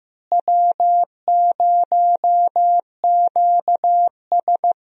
Morse code, heard as a keyed tone, about 15 wpm, 700 Hz. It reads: W0QS
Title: USA Call Signs - 4 Characters - Repeated 3x - 15wpm